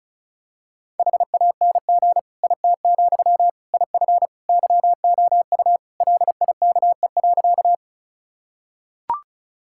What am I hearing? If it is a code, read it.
Morse code, heard as a keyed tone, about 36 wpm, 700 Hz. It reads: HANG IT, IF YOU LIKE.
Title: Sentences from Top 1600 Words 35wpm